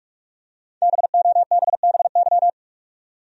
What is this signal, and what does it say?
Morse code, heard as a keyed tone, about 46 wpm, 700 Hz. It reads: BOBBY
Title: USA Common Names - 45wpm